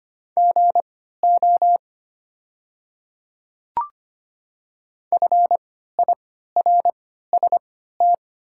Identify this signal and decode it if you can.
Morse code, heard as a keyed tone, about 25 wpm, spaced as for 14 wpm, 700 Hz. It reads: GO FIRST
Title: 3x Character Spacing - Top 200 Words - 25wpm